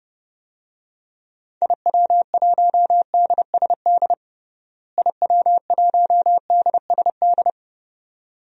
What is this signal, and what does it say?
Morse code, heard as a keyed tone, about 32 wpm, 700 Hz. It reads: IW1DSD IW1DSD
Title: DXCC-100 Call Signs - 6 Characters - Repeated 2x - 30wpm